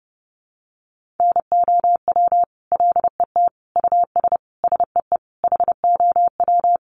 Morse code, heard as a keyed tone, about 30 wpm, 700 Hz. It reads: NOW LET US SEE HOW
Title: Sentences from Top 1400 Words 30wpm